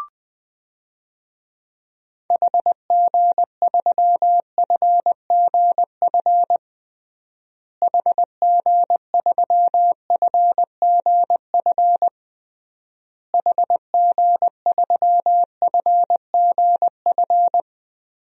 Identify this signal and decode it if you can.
Morse code, heard as a keyed tone, about 21 wpm, 700 Hz. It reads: HG3FGF HG3FGF HG3FGF
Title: DXCC-100 Call Signs - 6 Characters - Repeated 3x - 20wpm